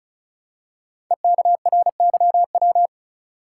Morse code, heard as a keyed tone, about 36 wpm, 700 Hz. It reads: EKRYW
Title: Code Groups - 5 Letters - 35wpm